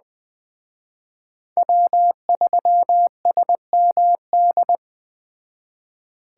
Morse code, heard as a keyed tone, about 20 wpm, 700 Hz. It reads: W3SMD